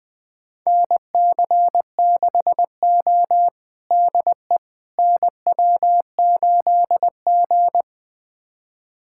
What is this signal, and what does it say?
Morse code, heard as a keyed tone, about 20 wpm, 700 Hz. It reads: NC6O DE NW8G
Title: USA Callsign DE Callsign - 4 Characters - 20wpm